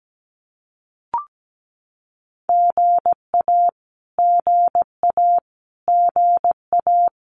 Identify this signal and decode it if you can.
Morse code, heard as a keyed tone, about 17 wpm, 700 Hz. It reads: GA GA GA
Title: Abbreviations Repeated 3x - 17wpm